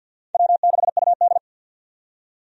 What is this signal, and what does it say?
Morse code, heard as a keyed tone, about 53 wpm, 700 Hz. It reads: W6UD
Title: USA Call Signs - 4 Characters - 50wpm